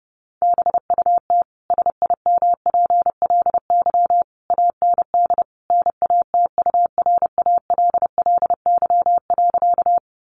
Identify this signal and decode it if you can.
Morse code, heard as a keyed tone, about 30 wpm, 700 Hz. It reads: BUT SIMPLY AND NATURALLY.
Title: Sentences from Top 1300 Words 30wpm